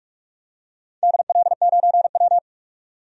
Morse code, heard as a keyed tone, about 46 wpm, 700 Hz. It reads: DL9W
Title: DXCC-100 Call Signs - 4 Characters - 45wpm